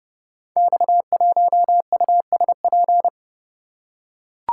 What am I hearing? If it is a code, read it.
Morse code, heard as a keyed tone, about 30 wpm, 700 Hz. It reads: X1USP
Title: Code Groups - 5 Letter:Numbers - 30wpm